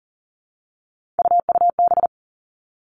Morse code, heard as a keyed tone, about 40 wpm, 700 Hz. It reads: UUB
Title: Three Challenging Letter-Number Combinations 40wpm